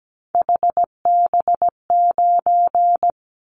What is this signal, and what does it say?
Morse code, heard as a keyed tone, about 17 wpm, 700 Hz. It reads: HB9
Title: Callsign Prefixes for Top 25 DXCC - Repeated 3x - 17wpm